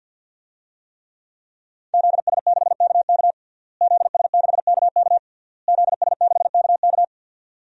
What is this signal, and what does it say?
Morse code, heard as a keyed tone, about 52 wpm, 700 Hz. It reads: ZS6KK ZS6KK ZS6KK
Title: DXCC-100 Call Signs - 5 Characters - Repeated 3x - 50wpm